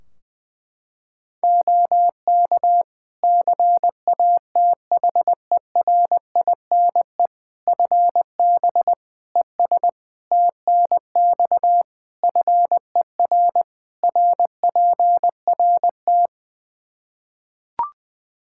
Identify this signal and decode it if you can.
Morse code, heard as a keyed tone, about 21 wpm, 700 Hz. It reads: OK CATHERINE FB ES TNX FER RPRT
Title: Practice QSO - Exchange 2 - Info - 20wpm